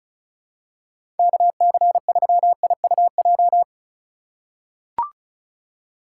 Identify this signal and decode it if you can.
Morse code, heard as a keyed tone, about 36 wpm, 700 Hz. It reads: KC3IUJ
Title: USA Call Signs - 6 Characters - 3x Character Spacing - 35wpm